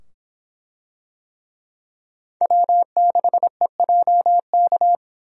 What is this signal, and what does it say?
Morse code, heard as a keyed tone, about 24 wpm, 700 Hz. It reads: W6EJK